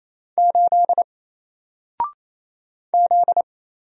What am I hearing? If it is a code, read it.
Morse code, heard as a keyed tone, about 28 wpm, 700 Hz. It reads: 8 Z
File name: Single-Letter-Number-28wpm